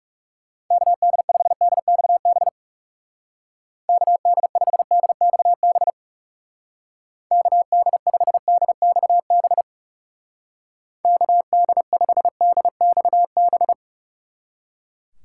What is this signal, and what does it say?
Morse code, heard as a keyed tone, about 36 wpm, 700 Hz. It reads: KD5DXB KD5DXB KD5DXB KD5DXB
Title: Speed Racing - USA 2x3 Call Signs 30wpm